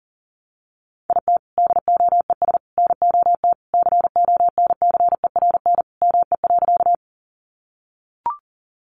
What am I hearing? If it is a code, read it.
Morse code, heard as a keyed tone, about 40 wpm, 700 Hz. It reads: IT DOES NOT CONCERN ME.
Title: Sentences from Top 1000 Words 40wpm